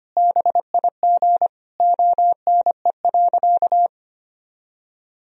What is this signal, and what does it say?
Morse code, heard as a keyed tone, about 25 wpm, 700 Hz. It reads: BIG ONE.
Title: Sentences from Top 1000 Words - Review Part 1 - 25wpm